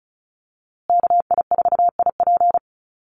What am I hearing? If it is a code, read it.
Morse code, heard as a keyed tone, about 34 wpm, 700 Hz. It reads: KI4IP
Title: USA Call Signs Repeated 3x 35wpm